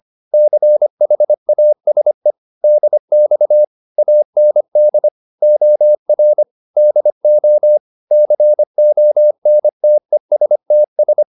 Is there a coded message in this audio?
Yes: Morse code from 0.3 to 11.2 s, heard as a keyed tone, about 25 wpm, 600 Hz. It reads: CHASE DX AND OR DO CONTESTS